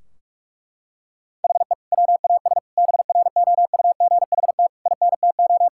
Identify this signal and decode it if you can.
Morse code, heard as a keyed tone, about 46 wpm, 700 Hz. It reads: HE WAS BROUGHT INTO
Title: Sentences from Top 1600 Words 45wpm